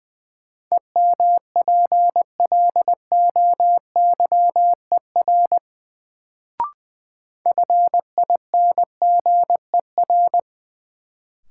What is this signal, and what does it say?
Morse code, heard as a keyed tone, about 21 wpm, 700 Hz. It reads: EMPLOYER FINGER
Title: Sentences from Top 1600 Words - New Word Drill Alt - 20wpm